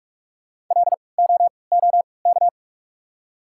Morse code, heard as a keyed tone, about 48 wpm, spaced as for 26 wpm, 700 Hz. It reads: LOOK
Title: 3x Character Spacing - Top 100 Words - 45wpm